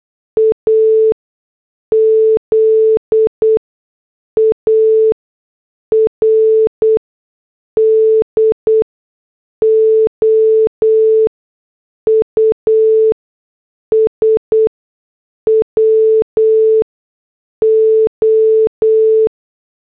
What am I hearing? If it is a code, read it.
Morse code, heard as a keyed tone, about 8 wpm, spaced as for 6 wpm, 440 Hz. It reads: AZARDOUS WO